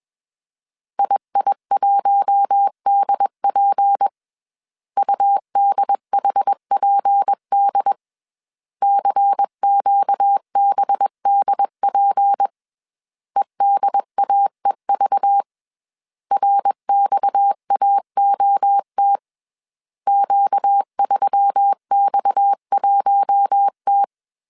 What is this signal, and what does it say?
Morse code, heard as a keyed tone, about 21 wpm, 800 Hz. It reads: II1DP UDHPD CQBDP EDAEV RXAOT Q3X1T
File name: CW / DET / 1 6 1500C S C 20wpm 2116